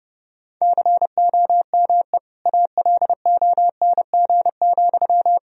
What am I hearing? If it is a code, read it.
Morse code, heard as a keyed tone, about 32 wpm, 700 Hz. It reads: COME ALONG,
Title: Sentences from Top 1000 Words - Review Part 1 - 30wpm